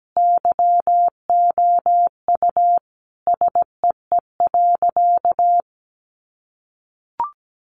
Morse code, heard as a keyed tone, about 17 wpm, 700 Hz. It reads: YOU SEE.
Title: Sentences from Top 600 Words 17wpm